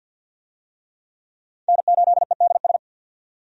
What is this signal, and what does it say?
Morse code, heard as a keyed tone, about 50 wpm, 700 Hz. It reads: N8EDS
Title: USA Call Signs - 5 Characters - 50wpm